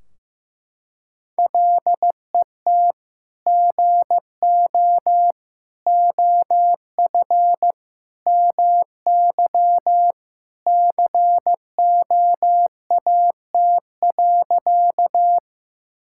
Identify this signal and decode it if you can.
Morse code, heard as a keyed tone, about 16 wpm, 700 Hz. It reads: LET GO OF MY COAT.
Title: Sentences from Top 1600 Words 15wpm